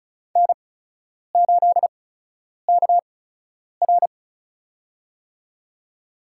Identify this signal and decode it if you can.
Morse code, heard as a keyed tone, about 36 wpm, spaced as for 10 wpm, 700 Hz. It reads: N8KR
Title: USA Call Signs - 8x Character Spacing - 35wpm